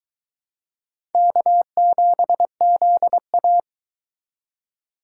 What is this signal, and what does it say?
Morse code, heard as a keyed tone, about 20 wpm, 700 Hz. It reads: K7ZA